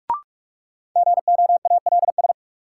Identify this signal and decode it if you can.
Morse code, heard as a keyed tone, about 46 wpm, 700 Hz. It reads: GOALS
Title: Sentences from Top 1000 Words - Review Part 6 - New Word Drill - 45wpm